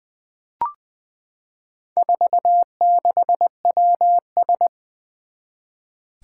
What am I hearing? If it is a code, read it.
Morse code, heard as a keyed tone, about 21 wpm, 700 Hz. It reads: 46WS
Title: Four Challenging Letter-Number Combinations 20wpm